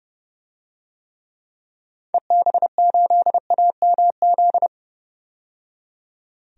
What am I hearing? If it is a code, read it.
Morse code, heard as a keyed tone, about 30 wpm, 700 Hz. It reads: EB8AMZ
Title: DXCC-100 Call Signs - 6 Characters - 30wpm